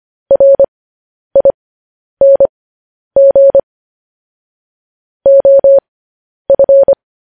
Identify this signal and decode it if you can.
Morse code, heard as a keyed tone, about 25 wpm, spaced as for 10 wpm, 550 Hz. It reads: RING OF
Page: news headlines in morse code at 10 wpm